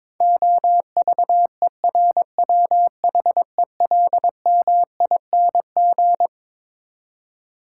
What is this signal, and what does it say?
Morse code, heard as a keyed tone, about 22 wpm, 700 Hz. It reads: OVERWHELMING
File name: Twelve Letter Words - 22wpm